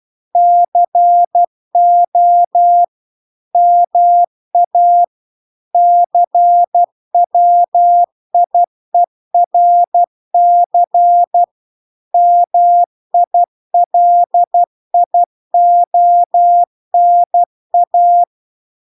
Morse code, heard as a keyed tone, about 12 wpm, 700 Hz. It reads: CO MA CWIERC MILIONA